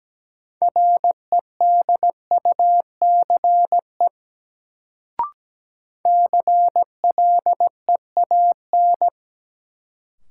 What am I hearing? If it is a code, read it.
Morse code, heard as a keyed tone, about 17 wpm, 700 Hz. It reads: REDUCE CLEAN
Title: Sentences from Top 1100 Words - New Word Drill - 17wpm